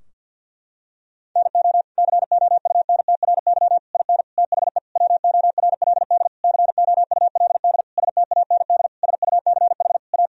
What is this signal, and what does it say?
Morse code, heard as a keyed tone, about 50 wpm, 700 Hz. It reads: NO COUNTRY IN THE WORLD COULD STAND SUCH A
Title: Sentences from Top 1000 Words - Review Part 3 - 50wpm